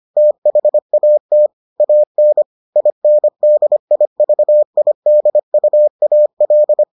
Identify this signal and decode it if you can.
Morse code, heard as a keyed tone, about 25 wpm, 600 Hz. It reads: THAT AN INDIVIDUAL